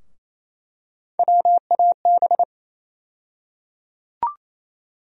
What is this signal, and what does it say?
Morse code, heard as a keyed tone, about 28 wpm, 700 Hz. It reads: WAB
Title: USA Call Sign Postfixes - 3 Letters 28wpm